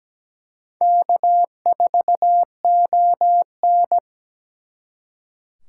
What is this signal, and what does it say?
Morse code, heard as a keyed tone, about 20 wpm, 700 Hz. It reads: K4ON
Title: Speed Racing - USA 1x2 Call Signs 17wpm